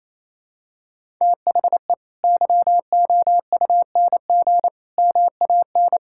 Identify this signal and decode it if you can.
Morse code, heard as a keyed tone, about 28 wpm, 700 Hz. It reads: THE YOUNG MAN
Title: Sentences from Top 1000 Words - Review Part 6 - 28wpm